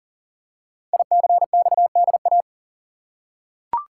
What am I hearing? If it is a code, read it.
Morse code, heard as a keyed tone, about 39 wpm, 700 Hz. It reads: ICXDA
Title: Code Groups - 5 Letter:Numbers - 40wpm